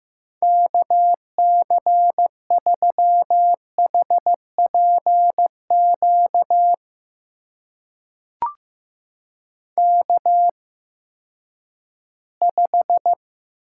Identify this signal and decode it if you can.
Morse code, heard as a keyed tone, about 15 wpm, spaced as for 10 wpm, 700 Hz. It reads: KC3HPQ K5
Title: USA Call Signs - 8x Character Spacing - 15wpm